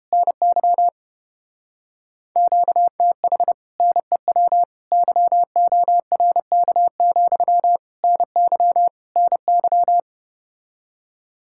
Morse code, heard as a keyed tone, about 33 wpm, 700 Hz. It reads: NY QTH NEW YORK, NY NY